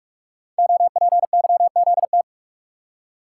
Morse code, heard as a keyed tone, about 44 wpm, 700 Hz. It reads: OPYZT
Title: Code Groups - 5 Letters - 45wpm